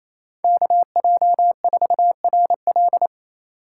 Morse code, heard as a keyed tone, about 34 wpm, 700 Hz. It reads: KJ4RL
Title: Speed Racing - USA 2x2 Call Signs 28wpm